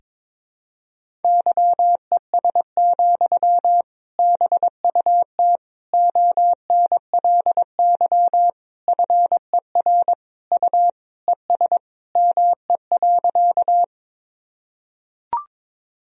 Morse code, heard as a keyed tone, about 22 wpm, 700 Hz. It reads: YES, BUT ONLY FER U ES ME.
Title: Sentences from Top 500 Words and Abbreviations 22wpm